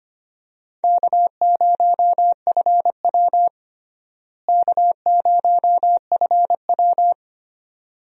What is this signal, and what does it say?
Morse code, heard as a keyed tone, about 25 wpm, 700 Hz. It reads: K0FW K0FW
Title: USA Call Signs - 4 Characters - Repeated 3x - 25wpm